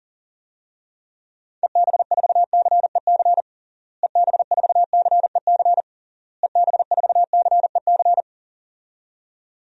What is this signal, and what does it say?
Morse code, heard as a keyed tone, about 37 wpm, 700 Hz. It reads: EB4CEC EB4CEC EB4CEC
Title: DXCC-100 Call Signs - 6 Characters - Repeated 3x - 40wpm